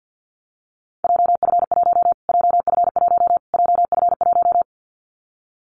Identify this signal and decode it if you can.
Morse code, heard as a keyed tone, about 50 wpm, 700 Hz. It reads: JF1 JF1 JF1
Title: Callsign Prefixes for Top 25 DXCC - Repeated 3x - 50wpm